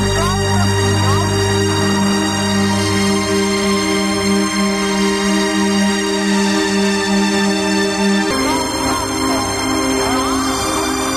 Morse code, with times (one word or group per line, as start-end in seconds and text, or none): none